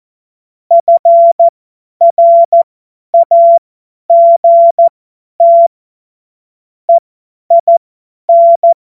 Morse code, heard as a keyed tone, about 14 wpm, spaced as for 10 wpm, 680 Hz. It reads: FRAGT EIN
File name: Witz-23-September-2025-50-F70